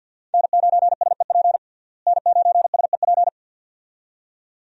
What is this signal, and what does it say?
Morse code, heard as a keyed tone, about 47 wpm, 700 Hz. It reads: N9SEP N9SEP
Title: USA Call Signs - 5 Characters - Repeated 2x - 50wpm